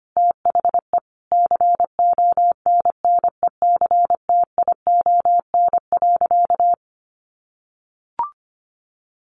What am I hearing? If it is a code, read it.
Morse code, heard as a keyed tone, about 25 wpm, 700 Hz. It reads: THE CONNECTION.
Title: Sentences from Top 1300 Words 25wpm